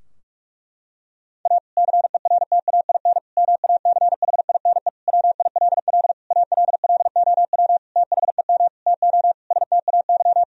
Morse code, heard as a keyed tone, about 46 wpm, 700 Hz. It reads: A CERTAIN MACHINE WILL ALLOW THEM TO STAY
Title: Sentences from Top 1000 Words - Review Part 2 - 45wpm